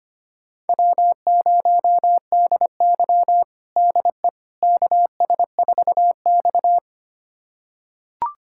Morse code, heard as a keyed tone, about 25 wpm, 700 Hz. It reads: W0DY DE KS4X
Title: USA Callsign DE Callsign - 4 Characters - 25wpm